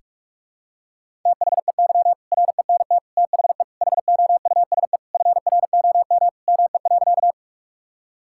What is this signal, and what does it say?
Morse code, heard as a keyed tone, about 41 wpm, 700 Hz. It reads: THEY RENT THE HOUSE FROM ME.
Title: Sentences from Top 1000 Words - Review Part 5 - 45wpm